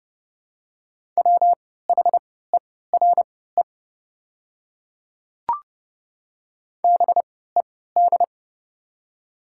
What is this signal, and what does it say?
Morse code, heard as a keyed tone, about 30 wpm, spaced as for 17 wpm, 700 Hz. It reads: WHERE BED
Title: 3x Character Spacing - Top 500 Words - 30wpm